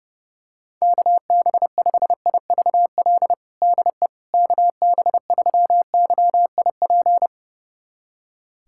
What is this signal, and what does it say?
Morse code, heard as a keyed tone, about 30 wpm, 700 Hz. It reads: KB5IVL DE KB3YIP
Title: USA Callsign DE Callsign - 6 Characters - 30wpm